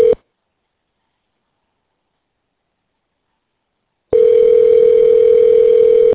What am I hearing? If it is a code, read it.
Ringback tone (North America), 440 + 480 Hz.